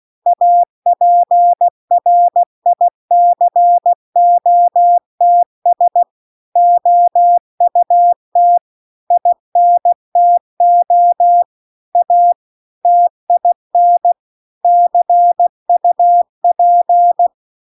Morse code, heard as a keyed tone, about 16 wpm, 700 Hz. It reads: APRICOTS OUT INTO A TIN CUP